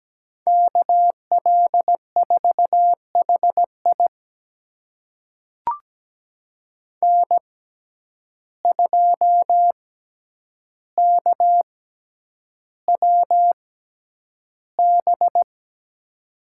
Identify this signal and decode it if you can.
Morse code, heard as a keyed tone, about 17 wpm, spaced as for 9 wpm, 700 Hz. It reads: KL4HI N2KWB
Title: USA Call Signs - 6x Character Spacing - 17wpm